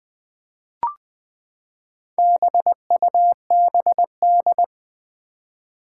Morse code, heard as a keyed tone, about 20 wpm, 700 Hz. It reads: BUBD
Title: Four Challenging Letter-Number Combinations 20wpm